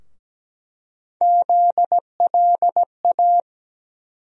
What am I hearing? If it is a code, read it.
Morse code, heard as a keyed tone, about 17 wpm, 700 Hz. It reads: ZLA
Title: USA Call Sign Postfixes - 3 Letters 17wpm